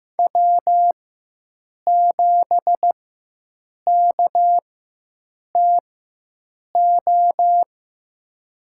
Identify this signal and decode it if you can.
Morse code, heard as a keyed tone, about 15 wpm, spaced as for 7 wpm, 700 Hz. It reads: W7KTO
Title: USA Call Signs - 5 Characters - 4x Character Spacing - 15wpm